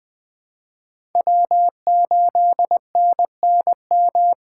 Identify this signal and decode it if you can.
Morse code, heard as a keyed tone, about 21 wpm, 700 Hz. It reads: W8NNM
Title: USA Call Signs - 5 Characters - 3x Character Spacing - 20wpm